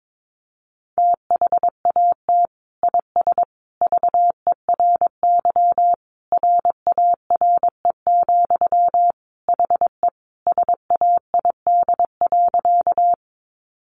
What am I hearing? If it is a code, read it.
Morse code, heard as a keyed tone, about 22 wpm, 700 Hz. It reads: THAT IS VERY RARE, HE SAID.
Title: Sentences from Top 1100 Words 22wpm